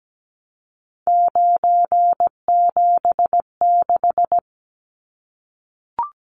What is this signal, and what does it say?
Morse code, heard as a keyed tone, about 17 wpm, 700 Hz. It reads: 976
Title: Numbers - 3-Digit 17wpm